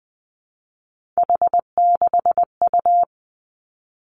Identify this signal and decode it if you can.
Morse code, heard as a keyed tone, about 20 wpm, 700 Hz. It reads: H6U
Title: Three Challenging Letter-Number Combinations 20wpm